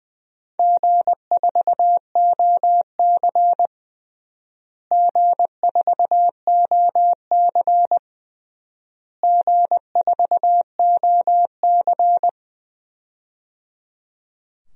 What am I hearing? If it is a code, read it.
Morse code, heard as a keyed tone, about 21 wpm, 700 Hz. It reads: G4OC G4OC G4OC
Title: DXCC-100 Call Signs - 4 Characters - Repeated 3x - 20wpm